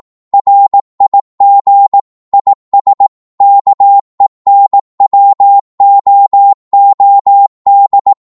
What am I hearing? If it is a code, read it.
Morse code, heard as a keyed tone, about 18 wpm, 800 Hz. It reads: RIG IS KENWOOD
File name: Morse / qso-18wpm